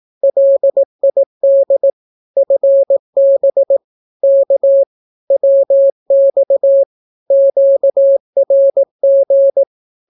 Morse code, heard as a keyed tone, about 18 wpm, 550 Hz. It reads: LID FB K WX QRG
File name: naukapodstaw / skroty / skroty